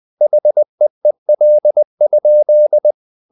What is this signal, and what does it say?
Morse code, heard as a keyed tone, about 20 wpm, 600 Hz. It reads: HEEL?
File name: the old man and the sea / the old man and the sea part 1 20wpm